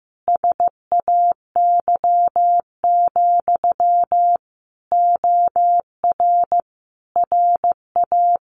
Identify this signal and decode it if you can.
Morse code, heard as a keyed tone, about 15 wpm, 700 Hz. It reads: SAY, OR RA
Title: Sentences from Top 700 Words 15wpm